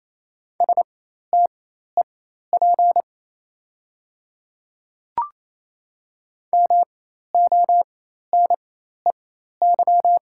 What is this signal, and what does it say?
Morse code, heard as a keyed tone, about 28 wpm, spaced as for 13 wpm, 690 Hz. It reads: STEP MONEY
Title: 4x Character Spacing - Top 500 Words - 28wpm